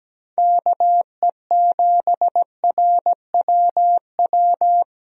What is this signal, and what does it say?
Morse code, heard as a keyed tone, about 17 wpm, 700 Hz. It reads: KE7RWW
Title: USA Call Signs - 6 Characters - Repeated 2x - 17wpm